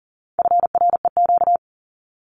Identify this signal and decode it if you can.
Morse code, heard as a keyed tone, about 40 wpm, 700 Hz. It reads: FREQ
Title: Abbreviations Repeated 3x - 40wpm